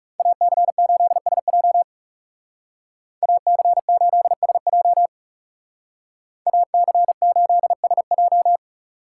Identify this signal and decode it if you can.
Morse code, heard as a keyed tone, about 39 wpm, 700 Hz. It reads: AC8SJ AC8SJ AC8SJ